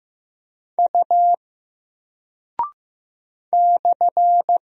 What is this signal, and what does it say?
Morse code, heard as a keyed tone, about 15 wpm, 700 Hz. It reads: U /
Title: Single Character (Letter:Number:Punctuation:Prosign) Alt 15wpm